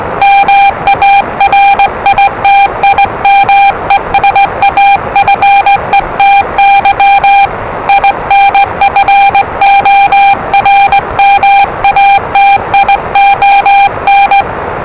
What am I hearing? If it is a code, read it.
Morse code, heard as a keyed tone, about 20 wpm, 800 Hz, with noise. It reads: MARITIME SAFETY INFORMATION
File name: WellingtonRadioZlwCloseDownOnHfAndMf / 01-Zlw 4277khz 417.5 qrt until 0925z